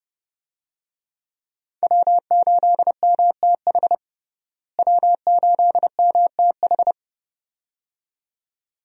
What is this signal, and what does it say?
Morse code, heard as a keyed tone, about 30 wpm, 700 Hz. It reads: W8MTH W8MTH
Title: USA Call Signs - 5 Characters - Repeated 2x - 30wpm